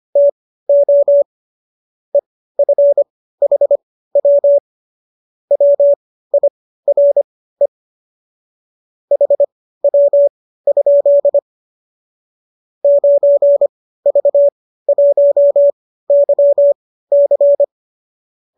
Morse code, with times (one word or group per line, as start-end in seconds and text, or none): 0.2-1.2 TO
2.1-4.6 EFHW
5.5-7.7 WIRE
9.1-11.4 HW?
12.8-17.6 9V1YC